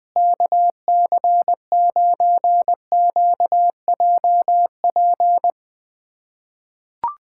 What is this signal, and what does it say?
Morse code, heard as a keyed tone, about 20 wpm, 700 Hz. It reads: KC9QJP
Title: USA Callsign DE Callsign - 6 Characters - 20wpm